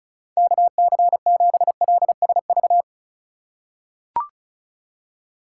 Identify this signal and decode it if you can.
Morse code, heard as a keyed tone, about 36 wpm, 700 Hz. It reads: KC7LSV
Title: USA Call Signs - 6 Characters - 4x Character Spacing - 35wpm